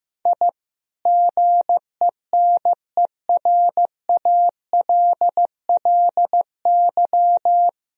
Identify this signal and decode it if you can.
Morse code, heard as a keyed tone, about 16 wpm, 700 Hz. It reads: I GENERALLY
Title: Sentences from Top 1000 Words - Review Part 4 - 15wpm